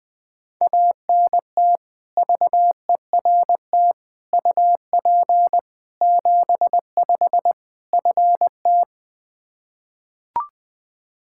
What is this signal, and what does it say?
Morse code, heard as a keyed tone, about 20 wpm, 700 Hz. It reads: ANT VERT UP 75 FT
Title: K1USN SST QSO Practice 20wpm